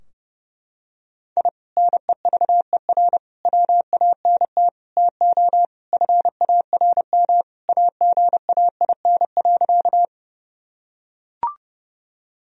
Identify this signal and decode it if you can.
Morse code, heard as a keyed tone, about 30 wpm, 700 Hz. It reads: I NEVER WANT TO FARM AGAIN.
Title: Sentences from Top 1000 Words - Review Part 2 - 30wpm